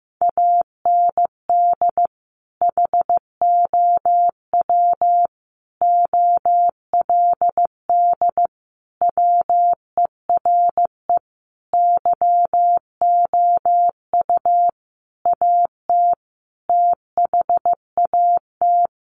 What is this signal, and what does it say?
Morse code, heard as a keyed tone, about 15 wpm, 700 Hz. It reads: AND HOW OLD WERE YOU AT THAT
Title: Sentences from Top 200 Words 15wpm